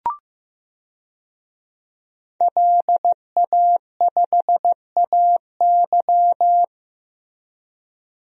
Morse code, heard as a keyed tone, about 15 wpm, 700 Hz. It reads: LA5AY
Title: DXCC-100 Call Signs - 5 Characters - Repeated 2x - 15wpm